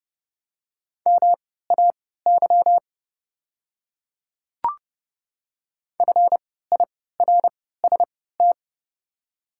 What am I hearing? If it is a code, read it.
Morse code, heard as a keyed tone, about 31 wpm, spaced as for 17 wpm, 700 Hz. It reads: MAY FIRST
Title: 3x Character Spacing - Top 100 Words - 30wpm